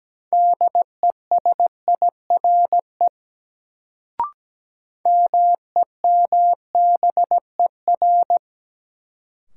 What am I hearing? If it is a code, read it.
Morse code, heard as a keyed tone, about 17 wpm, 700 Hz. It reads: DESIRE MEMBER
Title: Sentences from Top 1100 Words - New Word Drill - 17wpm